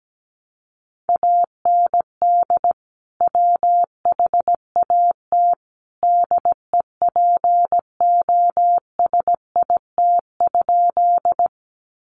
Morse code, heard as a keyed tone, about 17 wpm, 700 Hz. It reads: AND WHAT DEPOSIT?
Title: Sentences from Top 1300 Words 17wpm